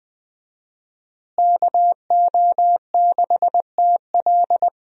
Morse code, heard as a keyed tone, about 21 wpm, 700 Hz. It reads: KO6TL
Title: USA Call Signs - 5 Characters - Repeated 2x - 20wpm